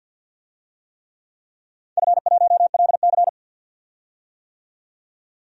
Morse code, heard as a keyed tone, about 47 wpm, 700 Hz. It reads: F1LC
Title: DXCC-100 Call Signs - 4 Characters - 50wpm